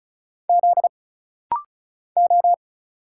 Morse code, heard as a keyed tone, about 36 wpm, 700 Hz. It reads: Z O
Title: Single Letter-Number Alt - Rapid-Fire 35wpm